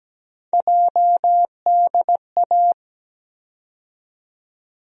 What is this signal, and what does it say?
Morse code, heard as a keyed tone, about 17 wpm, 700 Hz. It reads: JDA
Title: USA Call Sign Postfixes - 3 Letters 17wpm